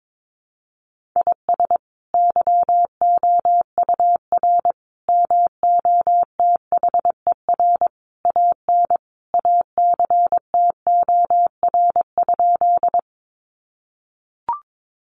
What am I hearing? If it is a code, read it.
Morse code, heard as a keyed tone, about 22 wpm, 700 Hz. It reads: IS YOUR MOTHER AN ACTOR?
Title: Sentences from Top 1500 Words 22wpm